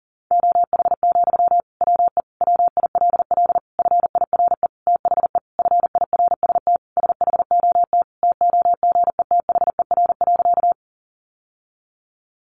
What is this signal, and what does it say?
Morse code, heard as a keyed tone, about 40 wpm, 700 Hz. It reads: OH, WE WILL FIRE THE FIRST SHOT TOGETHER.
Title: Sentences from Top 1100 Words 40wpm